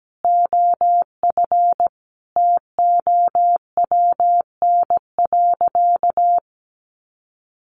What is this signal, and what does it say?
Morse code, heard as a keyed tone, about 17 wpm, 700 Hz. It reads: OF TOWN.
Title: Sentences from Top 1500 Words 17wpm